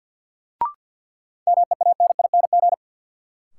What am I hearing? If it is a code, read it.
Morse code, heard as a keyed tone, about 51 wpm, 700 Hz. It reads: MEANING
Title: Sentences from Top 900 Words - New Word Drill - 50wpm